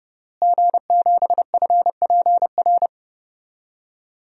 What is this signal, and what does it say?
Morse code, heard as a keyed tone, about 30 wpm, 700 Hz. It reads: G7FPR
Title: DXCC-100 Call Signs - 5 Characters - Repeated 2x - 30wpm